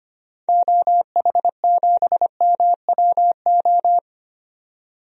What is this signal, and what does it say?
Morse code, heard as a keyed tone, about 25 wpm, 700 Hz. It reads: OH7MWO
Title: DXCC-100 Call Signs - 6 Characters - 25wpm